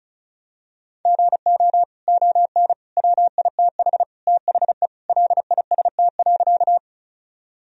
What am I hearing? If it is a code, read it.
Morse code, heard as a keyed tone, about 36 wpm, 700 Hz. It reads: GO ON WITH THE LIST.
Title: Sentences from Top 1000 Words - Review Part 2 - 35wpm